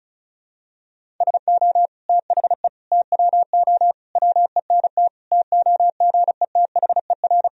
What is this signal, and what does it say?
Morse code, heard as a keyed tone, about 34 wpm, 690 Hz. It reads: SO THE TWO WENT TOGETHER
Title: Sentences from Top 1000 Words - Review Part 1 - 35wpm